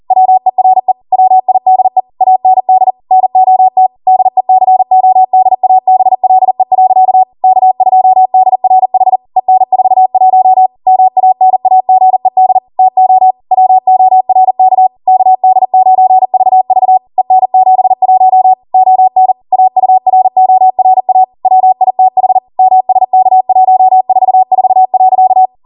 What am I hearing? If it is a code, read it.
Morse code, heard as a keyed tone, about 40 wpm, 750 Hz. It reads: WERE WIDE AND NOT DECODABLE. K2DRH EN41 MANAGED TO WORK KD9VV EN71 ON AURORA WITH MSK144.